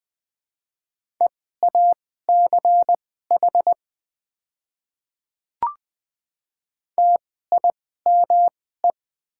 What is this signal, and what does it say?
Morse code, heard as a keyed tone, about 20 wpm, spaced as for 14 wpm, 700 Hz. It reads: EACH TIME
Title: 2x Character Spacing - Top 100 Words - 20wpm